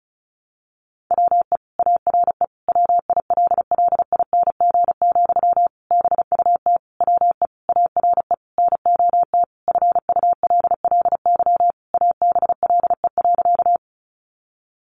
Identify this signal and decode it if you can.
Morse code, heard as a keyed tone, about 34 wpm, 700 Hz. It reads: WE ARE WILLING, BUT WE ARE NOT FULLY ABLE.
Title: Sentences from Top 1300 Words 35wpm